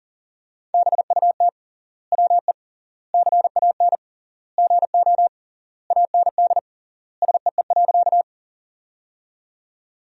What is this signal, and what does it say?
Morse code, heard as a keyed tone, about 40 wpm, 700 Hz. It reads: BUT WE CAN GO AND SEE.
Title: 3x Word-Spacing - Sentences from Top 100 Words - 40wpm